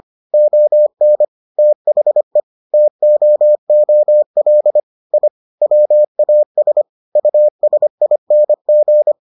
Morse code, heard as a keyed tone, about 25 wpm, 600 Hz. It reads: ON THE TOOL I WAS USING